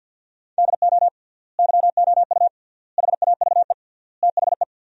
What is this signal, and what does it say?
Morse code, heard as a keyed tone, about 52 wpm, 700 Hz. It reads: DO YOU HAVE THE
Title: 3x Word-Spacing - Sentences from Top 300 Words - 50wpm